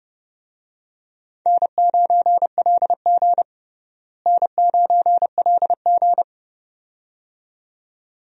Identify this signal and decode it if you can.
Morse code, heard as a keyed tone, about 30 wpm, 700 Hz. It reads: N9LG N9LG
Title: USA Call Signs - 4 Characters - Repeated 2x - 30wpm